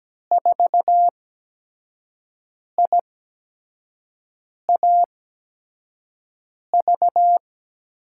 Morse code, heard as a keyed tone, about 17 wpm, spaced as for 5 wpm, 700 Hz. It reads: 4IAV
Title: USA Call Signs - 6 Characters - 8x Character Spacing - 17wpm